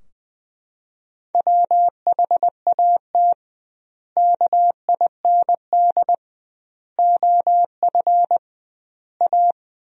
Morse code, heard as a keyed tone, about 21 wpm, 700 Hz. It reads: WHAT KIND OF A
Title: 2x Word-Spacing - Sentences from Top 400 Words - 20wpm